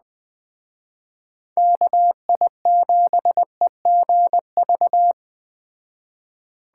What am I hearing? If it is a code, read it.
Morse code, heard as a keyed tone, about 21 wpm, 700 Hz. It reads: KI7EGV